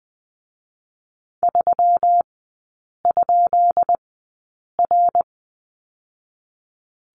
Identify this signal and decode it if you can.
Morse code, heard as a keyed tone, about 20 wpm, spaced as for 8 wpm, 700 Hz. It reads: 3?R
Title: 218 Lesson 35e - Number 0 - Three Characters 20wpm